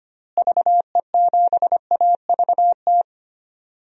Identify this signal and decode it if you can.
Morse code, heard as a keyed tone, about 25 wpm, 700 Hz. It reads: VE7AVT
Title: DXCC-100 Call Signs - 6 Characters - Repeated 3x - 25wpm